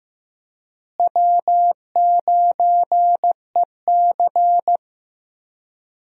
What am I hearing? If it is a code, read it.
Morse code, heard as a keyed tone, about 16 wpm, 700 Hz. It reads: W9EC
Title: USA Call Signs - 4 Characters - Repeated 2x - 15wpm